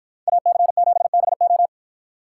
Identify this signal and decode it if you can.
Morse code, heard as a keyed tone, about 53 wpm, 700 Hz. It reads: AC7BO